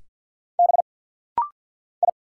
Morse code, heard as a keyed tone, about 50 wpm, 690 Hz. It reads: B I
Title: Single Letters - Rapid-Fire 50wpm